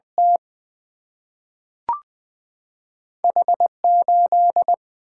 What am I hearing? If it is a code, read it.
Morse code, heard as a keyed tone, about 20 wpm, 700 Hz. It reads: T H8
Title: Single-Letter Single-Number 20wpm